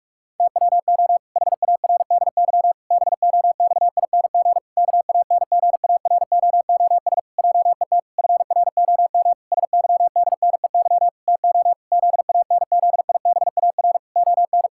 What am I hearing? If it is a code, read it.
Morse code, heard as a keyed tone, about 46 wpm, 700 Hz. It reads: TWO HARDY BOXING KANGAROOS JET FROM SYDNEY TO ZANZIBAR ON
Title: Pangram Challenge v2 - 45wpm